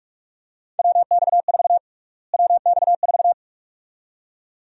Morse code, heard as a keyed tone, about 46 wpm, 700 Hz. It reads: WX4 WX4
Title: USA Call Sign Prefixes - Repeated 2x - 45wpm